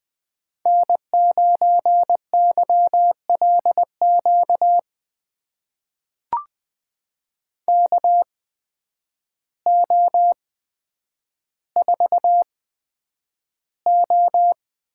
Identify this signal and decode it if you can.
Morse code, heard as a keyed tone, about 20 wpm, spaced as for 9 wpm, 700 Hz. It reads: N9YLQ KO4O
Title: USA Call Signs - 5 Characters - 8x Character Spacing - 20wpm